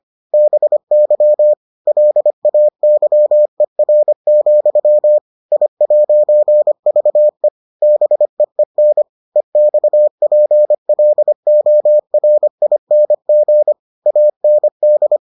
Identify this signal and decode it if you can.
Morse code, heard as a keyed tone, about 25 wpm, 600 Hz. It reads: BY LAYER, I'VE BEEN EXPLORING AND